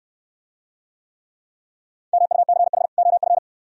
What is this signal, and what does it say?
Morse code, heard as a keyed tone, about 69 wpm, 700 Hz. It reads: MUCH OF